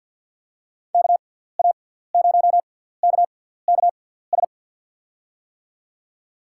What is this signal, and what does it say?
Morse code, heard as a keyed tone, about 54 wpm, spaced as for 17 wpm, 700 Hz. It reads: KA0KKS
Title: USA Call Signs - 6x Character Spacing - 50wpm